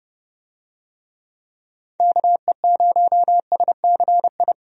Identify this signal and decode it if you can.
Morse code, heard as a keyed tone, about 30 wpm, 700 Hz. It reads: KE0SCI
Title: USA Call Signs - 6 Characters - Repeated 2x - 30wpm